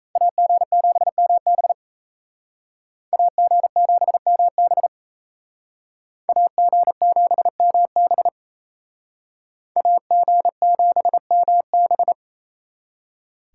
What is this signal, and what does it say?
Morse code, heard as a keyed tone, about 36 wpm, 700 Hz. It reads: AG7MB AG7MB AG7MB AG7MB